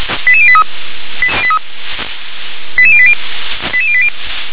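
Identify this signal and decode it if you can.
Five-tone selective call (ZVEI), heard as quick runs of five tones, tones of about 70 ms. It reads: 88083 88080 88080
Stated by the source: ZVEI five-tone selective call 88758 received on an SDR.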